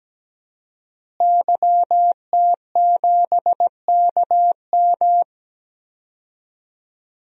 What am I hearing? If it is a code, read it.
Morse code, heard as a keyed tone, about 17 wpm, 700 Hz. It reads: YT7KM
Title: DXCC-100 Call Signs - 5 Characters - Repeated 2x - 17wpm